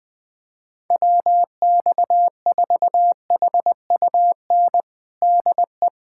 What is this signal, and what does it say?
Morse code, heard as a keyed tone, about 20 wpm, 700 Hz. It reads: WX4HUN DE
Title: USA Callsign DE Callsign - 6 Characters - 20wpm